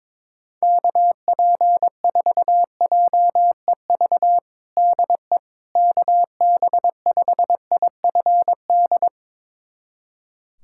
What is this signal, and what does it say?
Morse code, heard as a keyed tone, about 22 wpm, 700 Hz. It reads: KP4JEV DE KB5IFD
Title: USA Callsign DE Callsign 22wpm